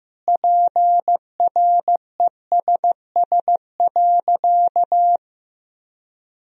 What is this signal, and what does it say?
Morse code, heard as a keyed tone, about 15 wpm, 700 Hz. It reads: PRESS.
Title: Sentences from Top 1000 Words - Review Part 1 - 15wpm